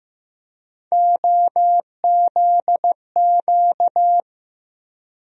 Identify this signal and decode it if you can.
Morse code, heard as a keyed tone, about 15 wpm, 700 Hz. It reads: OZQ
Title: USA Call Sign Postfixes - 3 Letters 15wpm